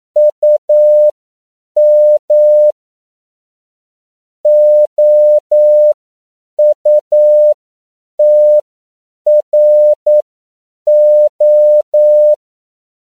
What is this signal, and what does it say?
Morse code, heard as a keyed tone, about 9 wpm, spaced as for 7 wpm, 600 Hz. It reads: UM OUTRO